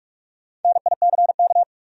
Morse code, heard as a keyed tone, about 46 wpm, 700 Hz. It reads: NICK
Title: USA Common Names - Max 4 Letters - 45wpm